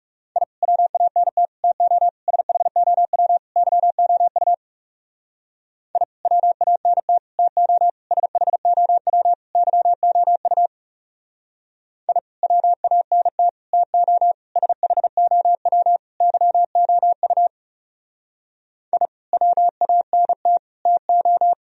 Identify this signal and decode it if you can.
Morse code, heard as a keyed tone, about 36 wpm, 700 Hz. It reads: I WANT TO SHOW YOU I WANT TO SHOW YOU I WANT TO SHOW YOU I WANT TO
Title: Speed-Racing - Sets of 5 Words 30wpm